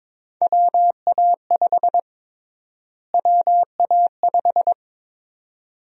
Morse code, heard as a keyed tone, about 22 wpm, 700 Hz. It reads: WA5 WA5
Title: USA Call Sign Prefixes - Repeated 2x - 22wpm